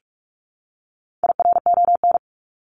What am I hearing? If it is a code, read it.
Morse code, heard as a keyed tone, about 44 wpm, 700 Hz. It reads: IRON
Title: Top 1751-2000 Words 45wpm